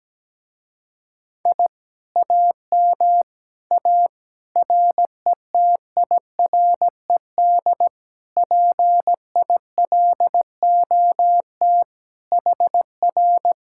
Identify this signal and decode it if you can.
Morse code, heard as a keyed tone, about 17 wpm, 700 Hz. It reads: I AM A RETIRED PILOT HR
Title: Speed-Racing Common QSO Phrases Encore - 17wpm